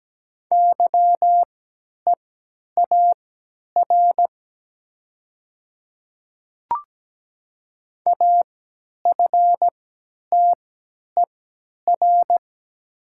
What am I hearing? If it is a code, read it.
Morse code, heard as a keyed tone, about 17 wpm, spaced as for 10 wpm, 700 Hz. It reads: YEAR AFTER
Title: 3x Character Spacing - Top 200 Words - 17wpm